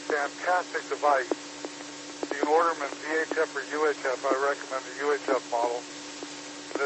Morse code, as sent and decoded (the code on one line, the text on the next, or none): none